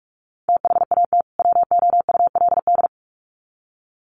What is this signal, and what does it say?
Morse code, heard as a keyed tone, about 44 wpm, 700 Hz. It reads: THAT WOULD
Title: Sets of 2 Words 45wpm